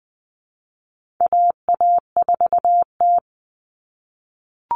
Morse code, heard as a keyed tone, about 20 wpm, 700 Hz. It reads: AA4T
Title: All Cut Numbers - 4-Digit 20wpm